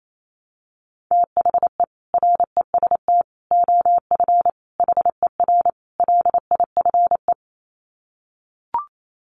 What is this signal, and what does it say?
Morse code, heard as a keyed tone, about 28 wpm, 700 Hz. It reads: THE REST OF HER LIFE
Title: Sets of 5 Words Encore 28wpm